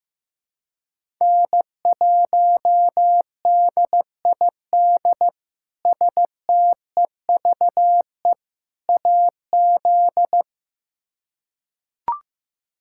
Morse code, heard as a keyed tone, about 15 wpm, 700 Hz. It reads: N1DID STEVE AZ
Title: K1USN SST QSO Practice - 15wpm